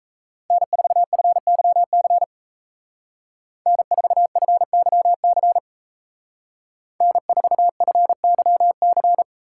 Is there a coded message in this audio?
Yes: Morse code from 0.5 to 9.2 s, heard as a keyed tone, about 37 wpm, 700 Hz. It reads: N4FYC N4FYC N4FYC